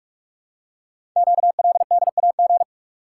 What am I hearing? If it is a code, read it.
Morse code, heard as a keyed tone, about 44 wpm, 690 Hz. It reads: QLDAG
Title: Code Groups - 5 Letters - 45wpm